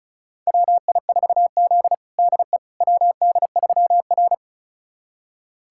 Morse code, heard as a keyed tone, about 36 wpm, 700 Hz. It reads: WI4Z DE WD3R
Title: USA Callsign DE Callsign - 4 Characters - 35wpm